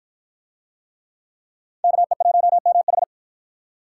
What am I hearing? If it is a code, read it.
Morse code, heard as a keyed tone, about 53 wpm, 700 Hz. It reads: KE1MH